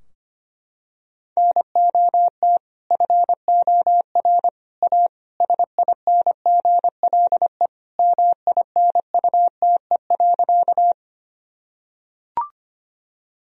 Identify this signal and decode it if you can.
Morse code, heard as a keyed tone, about 25 wpm, 700 Hz. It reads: NOT FOR A SINGLE MINUTE.
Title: Sentences from Top 1000 Words - Review Part 4 - 25wpm